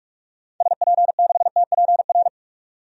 Morse code, heard as a keyed tone, about 46 wpm, 700 Hz. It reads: SP6TPR
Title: DXCC-100 Call Signs - 6 Characters - 45wpm